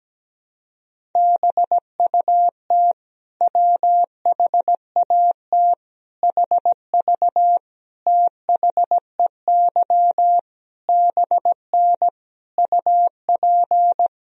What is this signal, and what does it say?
Morse code, heard as a keyed tone, about 17 wpm, 700 Hz. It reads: BUT WHAT HV THEY BN UP
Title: Sentences from Top 100 Words and Abbreviations 17wpm